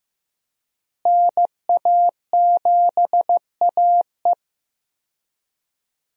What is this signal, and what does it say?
Morse code, heard as a keyed tone, about 15 wpm, 700 Hz. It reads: NA7AE
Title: USA Call Signs - 5 Characters - Repeated 2x - 15wpm